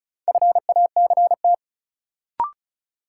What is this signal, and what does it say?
Morse code, heard as a keyed tone, about 36 wpm, 700 Hz. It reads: FACT
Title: Sentences from Top 500 Words - New Word Drill - 35wpm